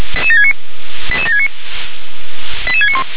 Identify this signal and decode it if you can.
Five-tone selective call (ZVEI), heard as quick runs of five tones, tones of about 70 ms. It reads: 88769 88691